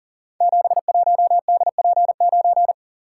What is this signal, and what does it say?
Morse code, heard as a keyed tone, about 38 wpm, 700 Hz. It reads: 71DP9
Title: Code Groups - 5 Letter:Numbers - 40wpm